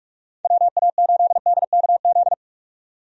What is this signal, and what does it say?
Morse code, heard as a keyed tone, about 46 wpm, 700 Hz. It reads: WA8DKZ
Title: USA Call Signs - 6 Characters - 45wpm